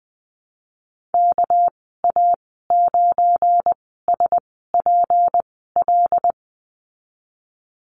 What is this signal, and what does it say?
Morse code, heard as a keyed tone, about 20 wpm, spaced as for 15 wpm, 700 Hz. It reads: KA9SPL
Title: 200 Lesson 32g - Letter k - Call Signs 20wpm